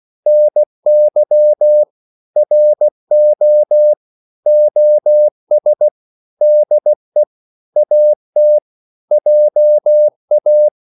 Morse code, heard as a keyed tone, about 16 wpm, 600 Hz. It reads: NY RO OS DE AT JA
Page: Korte danske ord 16wpm